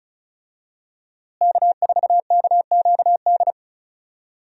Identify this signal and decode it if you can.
Morse code, heard as a keyed tone, about 36 wpm, 700 Hz. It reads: K4KQD
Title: USA Call Signs - 5 Characters - Repeated 3x - 35wpm